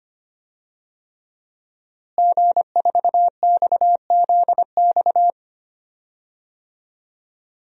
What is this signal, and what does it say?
Morse code, heard as a keyed tone, about 25 wpm, 700 Hz. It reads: G4XZX
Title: DXCC-100 Call Signs - 5 Characters - Repeated 3x - 25wpm